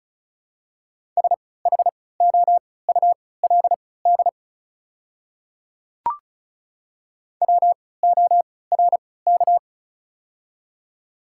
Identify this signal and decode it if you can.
Morse code, heard as a keyed tone, about 36 wpm, spaced as for 20 wpm, 700 Hz. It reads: SHOULD WORK
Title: 3x Character Spacing - Top 300 Words - 35wpm